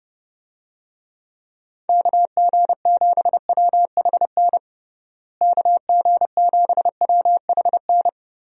Morse code, heard as a keyed tone, about 30 wpm, 700 Hz. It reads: KG7WHN KG7WHN
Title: USA Call Signs - 6 Characters - Repeated 2x - 30wpm